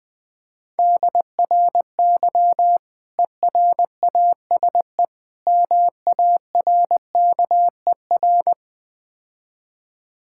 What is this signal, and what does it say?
Morse code, heard as a keyed tone, about 21 wpm, 700 Hz. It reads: DRY ERASE MARKER
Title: Easy Words - Speed-Racing with Hints - 20wpm